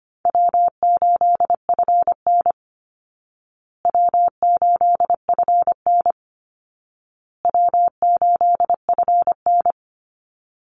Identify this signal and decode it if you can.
Morse code, heard as a keyed tone, about 25 wpm, 700 Hz. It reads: W8FN W8FN W8FN
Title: Call Signs Repeated 2x - 25wpm